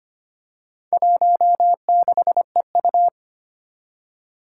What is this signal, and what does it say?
Morse code, heard as a keyed tone, about 25 wpm, 700 Hz. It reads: 16EU
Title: Four Challenging Letter-Number Combinations 25wpm